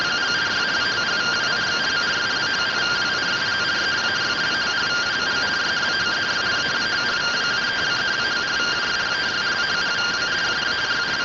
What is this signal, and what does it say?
Two-tone teleprinter signal, about 170 Hz shift.